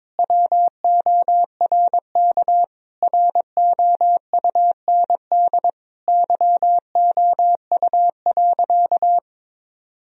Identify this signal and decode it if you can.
Morse code, heard as a keyed tone, about 22 wpm, 700 Hz. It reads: WORK ROUND YOU.
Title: Sentences from Top 1000 Words - Review Part 1 - 22wpm